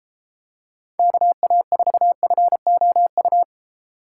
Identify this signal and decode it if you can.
Morse code, heard as a keyed tone, about 34 wpm, 700 Hz. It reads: KA4FOU